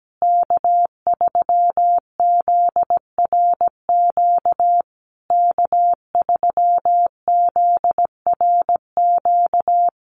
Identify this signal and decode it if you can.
Morse code, heard as a keyed tone, about 17 wpm, 700 Hz. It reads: K3ZRQ K3ZRQ
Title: USA Call Signs Repeated 3x 17wpm